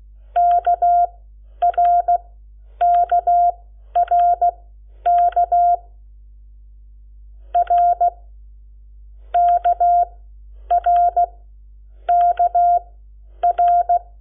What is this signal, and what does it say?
Morse code, heard as a keyed tone, about 16 wpm, spaced as for 9 wpm, 700 Hz. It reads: KRKRK RKRKR
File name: Lecciones / CLASE Nº7